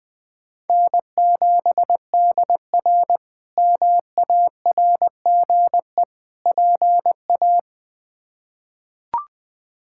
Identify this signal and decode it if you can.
Morse code, heard as a keyed tone, about 20 wpm, 700 Hz. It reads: N7DR MARGE PA
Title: K1USN SST QSO Practice - 20wpm